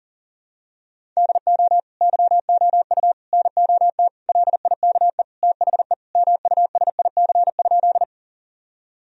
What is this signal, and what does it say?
Morse code, heard as a keyed tone, about 37 wpm, 700 Hz. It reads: DO YOU NOT LIKE THE MUSIC?
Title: Sentences from Top 1000 Words - Review Part 1 - 40wpm